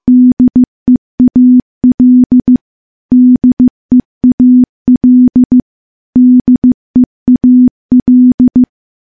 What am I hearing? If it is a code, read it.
Morse code, heard as a keyed tone, about 15 wpm, 250 Hz. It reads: DEAL DEAL DEAL
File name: morse-training-15WPM